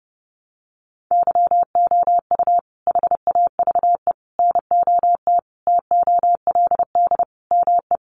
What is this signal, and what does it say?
Morse code, heard as a keyed tone, about 30 wpm, 700 Hz. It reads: YOU HAVE NOT TOLD ME
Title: Sentences from Top 1000 Words 30wpm